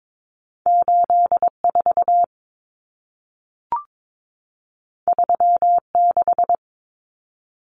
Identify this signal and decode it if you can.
Morse code, heard as a keyed tone, about 22 wpm, 700 Hz. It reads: 84 36
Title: Common Cut Numbers - 2-Digit 22wpm